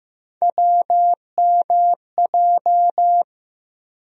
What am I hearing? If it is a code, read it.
Morse code, heard as a keyed tone, about 15 wpm, 700 Hz. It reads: WMJ
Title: USA Call Sign Postfixes - 3 Letters 15wpm